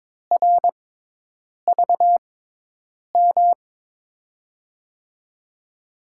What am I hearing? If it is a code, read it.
Morse code, heard as a keyed tone, about 22 wpm, spaced as for 8 wpm, 700 Hz. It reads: RVM
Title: USA Call Signs - 5 Characters - 6x Character Spacing - 22wpm